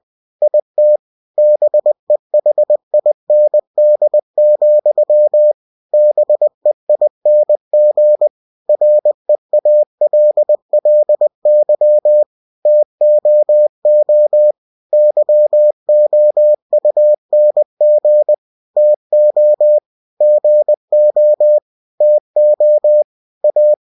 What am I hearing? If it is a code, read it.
Morse code, heard as a keyed tone, about 20 wpm, 600 Hz. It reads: IT BEHIND, BEING REALLY TOO YOUNG TO GO TO A
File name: cw-books / winnie-the-pooh / winnie-20wpm-page-0005